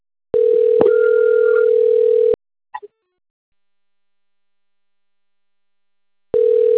Ringback tone (North America), 440 + 480 Hz.